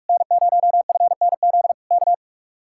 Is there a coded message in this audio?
Yes: Morse code from 0.1 to 2.1 s, heard as a keyed tone, about 47 wpm, 700 Hz. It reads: N0FNZ K